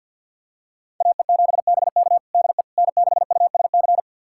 Morse code, heard as a keyed tone, about 47 wpm, 700 Hz. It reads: AE7BK DE N6USC
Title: USA Callsign DE Callsign - 5 Characters - 50wpm